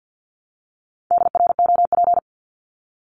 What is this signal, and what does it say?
Morse code, heard as a keyed tone, about 50 wpm, 700 Hz. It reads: DROP
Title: Top 1251-1500 Words 50wpm